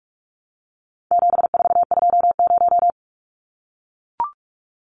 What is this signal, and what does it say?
Morse code, heard as a keyed tone, about 44 wpm, 700 Hz. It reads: 7420
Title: Numbers - 4-Digit 45wpm